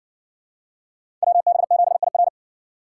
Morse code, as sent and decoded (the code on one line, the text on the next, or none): .-- -... --... .. .-.
WB7IR